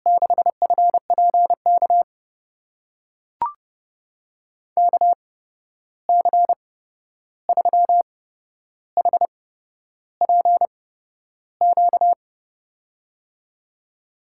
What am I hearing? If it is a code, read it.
Morse code, heard as a keyed tone, about 30 wpm, spaced as for 11 wpm, 700 Hz. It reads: 6FPK KC3HPQ
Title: USA Call Signs - 6 Characters - 8x Character Spacing - 30wpm